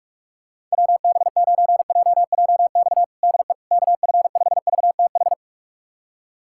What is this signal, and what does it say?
Morse code, heard as a keyed tone, about 41 wpm, 700 Hz. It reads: WB9JJX DE KF5VTH
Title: USA Callsign DE Callsign - 6 Characters - 45wpm